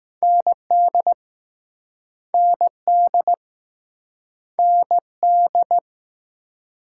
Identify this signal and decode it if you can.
Morse code, heard as a keyed tone, about 20 wpm, 700 Hz. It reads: ND ND ND